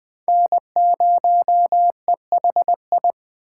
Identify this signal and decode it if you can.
Morse code, heard as a keyed tone, about 20 wpm, 700 Hz. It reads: N0EHI
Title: Speed Racing - USA 1x3 Call Signs 17wpm